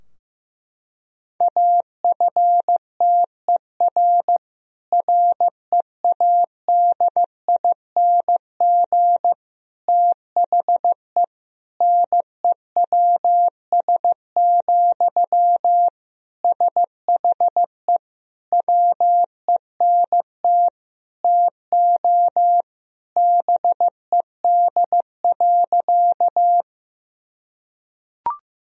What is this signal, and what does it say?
Morse code, heard as a keyed tone, about 15 wpm, 700 Hz. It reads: AFTER READING THE NEWS, SHE WENT TO BED.
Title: Sentences from Top 1000 Words - Review Part 3 - 15wpm